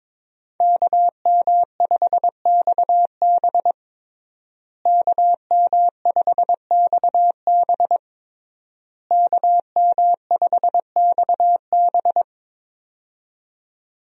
Morse code, heard as a keyed tone, about 23 wpm, 700 Hz. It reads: KM5XB KM5XB KM5XB
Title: USA Call Signs - 5 Characters - Repeated 3x - 22wpm